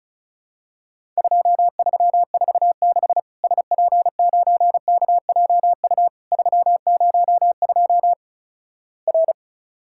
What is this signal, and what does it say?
Morse code, heard as a keyed tone, about 36 wpm, 700 Hz. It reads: 2346 SP9KJU 302 R
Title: Worked All Europe DX Contest - QTC Exchange - 35wpm